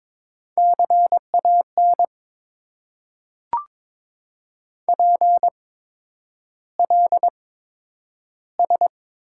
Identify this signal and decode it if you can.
Morse code, heard as a keyed tone, about 22 wpm, spaced as for 9 wpm, 690 Hz. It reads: CAN PLS
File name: Top 90 QSO Elements - 8x Character Spacing - 22wpm